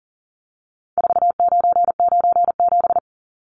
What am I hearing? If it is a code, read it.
Morse code, heard as a keyed tone, about 40 wpm, 700 Hz. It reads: 4997
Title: Numbers - 4-Digit 40wpm